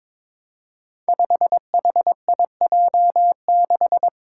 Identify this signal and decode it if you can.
Morse code, heard as a keyed tone, about 22 wpm, 700 Hz. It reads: 5HIJ6
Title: Five Challenging Letter-Number Combinations 22wpm